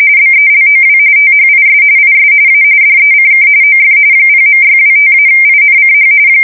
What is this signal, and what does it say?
RTTY, about 45.45 baud. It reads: BELOW THIS IS A DECK OF 0.625-INCH
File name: fsk170 / USS-Missouri